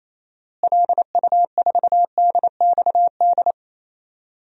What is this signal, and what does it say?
Morse code, heard as a keyed tone, about 28 wpm, 700 Hz. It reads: LU4DXD
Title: DXCC-100 Call Signs - 6 Characters - Repeated 3x - 28wpm